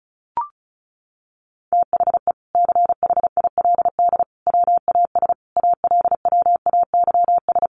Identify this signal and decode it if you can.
Morse code, heard as a keyed tone, about 34 wpm, 700 Hz. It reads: THE CHILD WAS ALWAYS
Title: Sentences from Top 1400 Words 35wpm